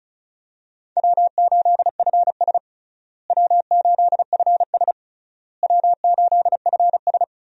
Morse code, heard as a keyed tone, about 36 wpm, 700 Hz. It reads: W8FS W8FS W8FS
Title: USA Call Signs - 4 Characters - Repeated 3x - 35wpm